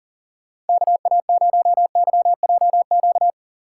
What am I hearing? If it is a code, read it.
Morse code, heard as a keyed tone, about 50 wpm, 700 Hz. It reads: KA0YJQ
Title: Speed Racing - USA Call Signs 40wpm